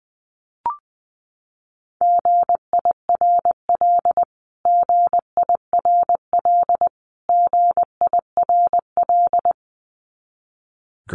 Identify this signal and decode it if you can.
Morse code, heard as a keyed tone, about 19 wpm, 700 Hz. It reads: GIRL GIRL GIRL
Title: Top 500 Words Repeated 20wpm